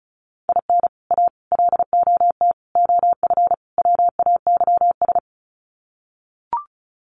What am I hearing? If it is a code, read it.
Morse code, heard as a keyed tone, about 34 wpm, 700 Hz. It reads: IN A LOT OF WAYS
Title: Sets of 5 Words Encore 35wpm